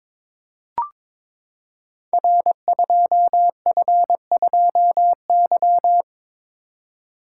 Morse code, heard as a keyed tone, about 22 wpm, 700 Hz. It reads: R2F2Y
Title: Code Groups - 5 Letter:Numbers - 22wpm